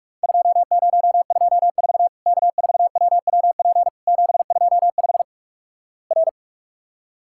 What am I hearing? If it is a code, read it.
Morse code, heard as a keyed tone, about 45 wpm, 700 Hz. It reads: 2024 K4WWP 725 R
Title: Worked All Europe DX Contest - QTC Exchange - 45wpm